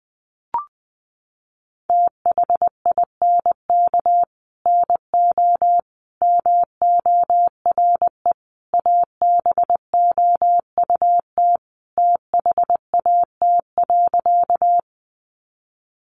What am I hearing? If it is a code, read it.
Morse code, heard as a keyed tone, about 20 wpm, 700 Hz. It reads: THINK NO MORE ABOUT THAT.
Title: Sentences from Top 200 Words 20wpm